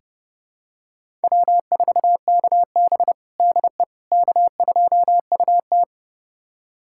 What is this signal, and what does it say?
Morse code, heard as a keyed tone, about 32 wpm, 700 Hz. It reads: W4KB DE K2UT
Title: USA Callsign DE Callsign - 4 Characters - 30wpm